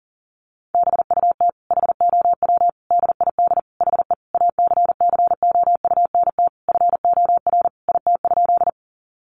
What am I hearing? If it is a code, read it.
Morse code, heard as a keyed tone, about 37 wpm, 700 Hz. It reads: BUT HOW DID HE ACCOUNT FOR IT?
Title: Sentences from Top 900 Words 40wpm